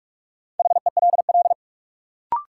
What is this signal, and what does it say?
Morse code, heard as a keyed tone, about 46 wpm, 700 Hz. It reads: HELL
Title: Sets of 5 Words Encore - New Word Drill - 45wpm